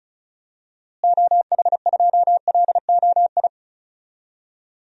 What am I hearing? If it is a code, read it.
Morse code, heard as a keyed tone, about 36 wpm, 700 Hz. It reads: OH2LOI
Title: DXCC-100 Call Signs - 6 Characters - Repeated 3x - 35wpm